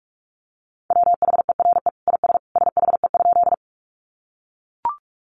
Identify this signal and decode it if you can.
Morse code, heard as a keyed tone, about 44 wpm, 700 Hz. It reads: WHERE IS SHE?
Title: Sentences from Top 200 Words 45wpm